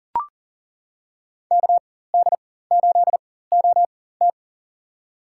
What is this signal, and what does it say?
Morse code, heard as a keyed tone, about 40 wpm, spaced as for 19 wpm, 700 Hz. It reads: KD8OT
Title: USA Call Signs - 4x Character Spacing - 40wpm